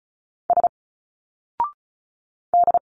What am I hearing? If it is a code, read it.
Morse code, heard as a keyed tone, about 34 wpm, 700 Hz. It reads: S D
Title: Single Letters 35wpm